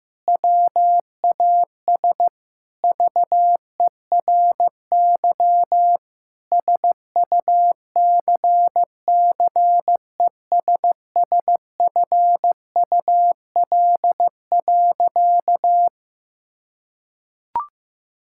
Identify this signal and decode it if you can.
Morse code, heard as a keyed tone, about 16 wpm, 700 Hz. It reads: WAS VERY SUCCESSFUL.
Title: Sentences from Top 1000 Words - Review Part 6 - 15wpm